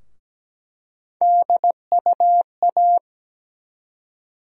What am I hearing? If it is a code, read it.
Morse code, heard as a keyed tone, about 17 wpm, 700 Hz. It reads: DUA
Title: USA Call Sign Postfixes - 3 Letters 17wpm